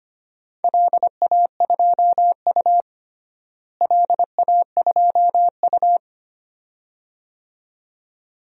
Morse code, heard as a keyed tone, about 25 wpm, 700 Hz. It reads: LA2U LA2U
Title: DXCC-100 Call Signs - 4 Characters - Repeated 2x - 25wpm